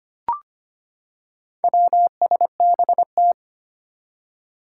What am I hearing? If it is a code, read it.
Morse code, heard as a keyed tone, about 25 wpm, 700 Hz. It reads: WSBT
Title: Four Challenging Letter-Number Combinations 25wpm